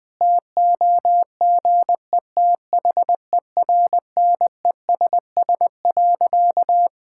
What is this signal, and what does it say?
Morse code, heard as a keyed tone, about 20 wpm, 700 Hz. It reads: TOGETHERNESS.
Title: Difficult Sentences with Hints - 20wpm